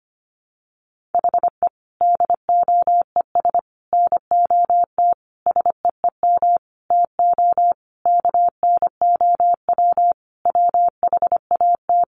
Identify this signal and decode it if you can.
Morse code, heard as a keyed tone, about 25 wpm, 700 Hz. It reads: HE DOES NOT SEEM TO KNOW WHAT
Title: Sentences from Top 1400 Words 25wpm